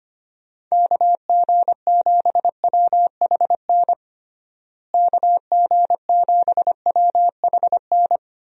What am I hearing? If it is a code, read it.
Morse code, heard as a keyed tone, about 25 wpm, 700 Hz. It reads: KG7WHN KG7WHN
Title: USA Call Signs - 6 Characters - Repeated 2x - 25wpm